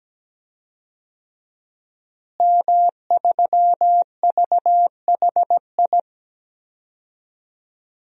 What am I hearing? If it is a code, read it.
Morse code, heard as a keyed tone, about 17 wpm, 700 Hz. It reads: M3VHI
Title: DXCC-100 Call Signs - 5 Characters - Repeated 2x - 17wpm